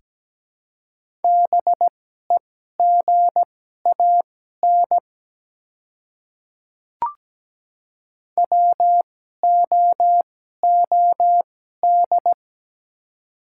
Morse code, heard as a keyed tone, about 17 wpm, spaced as for 12 wpm, 700 Hz. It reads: BEGAN WOOD
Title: 2x Character Spacing - Top 300 Words - 17wpm